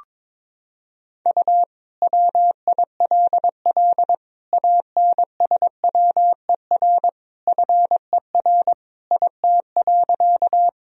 Morse code, heard as a keyed tone, about 22 wpm, 700 Hz. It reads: U WILL ANSWER FER IT.
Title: Sentences from Top 300 Words and Abbreviations 22wpm